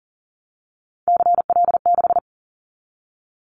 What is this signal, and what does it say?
Morse code, heard as a keyed tone, about 40 wpm, 700 Hz. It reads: CL6
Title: Callsign Prefixes for Top 100 DXCC - 40wpm